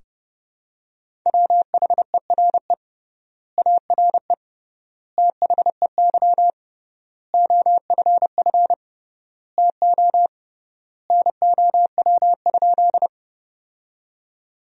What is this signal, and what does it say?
Morse code, heard as a keyed tone, about 31 wpm, 700 Hz. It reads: WHERE ARE THEY OFF TO NOW?
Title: 3x Word-Spacing - Sentences from Top 200 Words - 30wpm